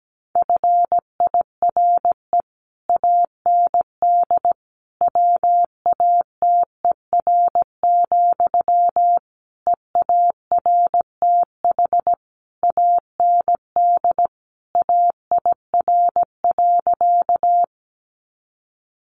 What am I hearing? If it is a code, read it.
Morse code, heard as a keyed tone, about 17 wpm, 700 Hz. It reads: FIRE AND WATER, EARTH AND AIR.
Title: Sentences from Top 400 Words 17wpm